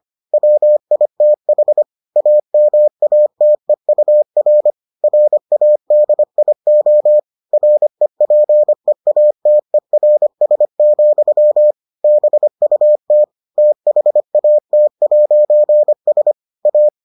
Morse code, heard as a keyed tone, about 25 wpm, 600 Hz. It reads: WITH AMATEUR RADIO REPEATERS, BUT THAT'S A